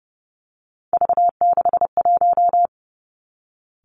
Morse code, heard as a keyed tone, about 30 wpm, 700 Hz. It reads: V61
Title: Three Challenging Letter-Number Combinations 30wpm